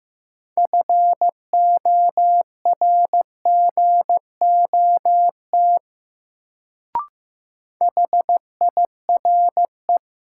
Morse code, heard as a keyed tone, about 15 wpm, 700 Hz. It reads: FORGOT HIRE
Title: Sentences from Top 1400 Words - New Word Drill Alt - 15wpm